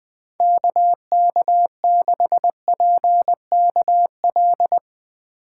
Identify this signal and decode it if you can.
Morse code, heard as a keyed tone, about 20 wpm, 700 Hz. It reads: KK6PKL
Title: USA Call Signs - 3x Character Spacing - 20wpm